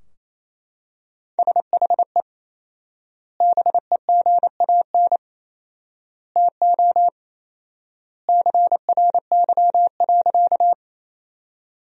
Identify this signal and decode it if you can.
Morse code, heard as a keyed tone, about 28 wpm, 700 Hz. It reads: SHE BEGAN TO CRY.
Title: 4x Word-Spacing - Sentences from Top 500 Words - 28wpm